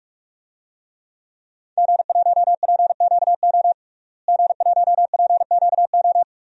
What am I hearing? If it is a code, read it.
Morse code, heard as a keyed tone, about 41 wpm, 700 Hz. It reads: G1PQO G1PQO
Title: DXCC-100 Call Signs - 5 Characters - Repeated 3x - 45wpm